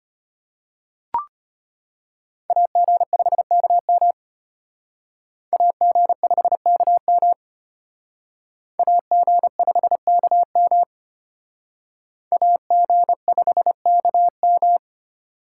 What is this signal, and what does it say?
Morse code, heard as a keyed tone, about 34 wpm, 700 Hz. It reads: AG5KM AG5KM AG5KM AG5KM